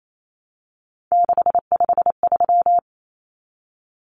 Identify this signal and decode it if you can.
Morse code, heard as a keyed tone, about 28 wpm, 700 Hz. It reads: 653
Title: Numbers - 3-Digit 28wpm